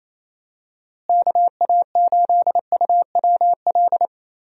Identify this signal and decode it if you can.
Morse code, heard as a keyed tone, about 28 wpm, 700 Hz. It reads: KA8UWL
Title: USA Call Signs - 6 Characters - Repeated 3x - 28wpm